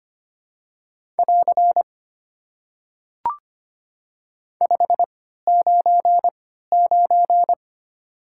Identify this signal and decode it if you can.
Morse code, heard as a keyed tone, about 25 wpm, 700 Hz. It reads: <AR> 5 9 9
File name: Top 90 QSO Elements - 3x Character Spacing - 25wpm